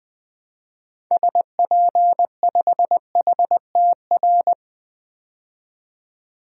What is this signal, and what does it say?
Morse code, heard as a keyed tone, about 20 wpm, 700 Hz. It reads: SP5HTR
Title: DXCC-100 Call Signs - 6 Characters - 20wpm